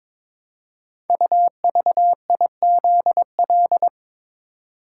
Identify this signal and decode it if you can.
Morse code, heard as a keyed tone, about 22 wpm, 700 Hz. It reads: UVIZL
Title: Code Groups - 5 Letters - 22wpm